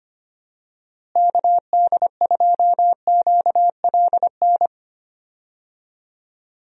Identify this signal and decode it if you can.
Morse code, heard as a keyed tone, about 25 wpm, 700 Hz. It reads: KD2QLN
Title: USA Call Signs - 6 Characters - Repeated 2x - 25wpm